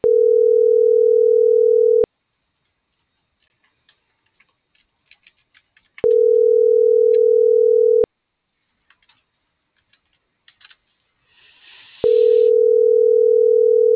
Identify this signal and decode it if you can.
Ringback tone (North America), 440 + 480 Hz.